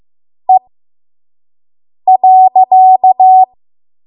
Morse code, heard as a keyed tone, about 15 wpm, spaced as for 5 wpm, 750 Hz. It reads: E.
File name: Morse / 05WPM